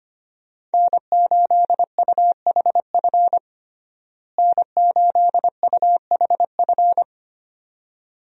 Morse code, heard as a keyed tone, about 25 wpm, 700 Hz. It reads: N8UHF N8UHF
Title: USA Call Signs - 5 Characters - Repeated 2x - 25wpm